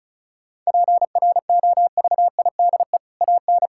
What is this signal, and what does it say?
Morse code, heard as a keyed tone, about 36 wpm, 700 Hz. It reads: PROVIDE AN
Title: Sentences from Top 1000 Words - Review Part 5 - 35wpm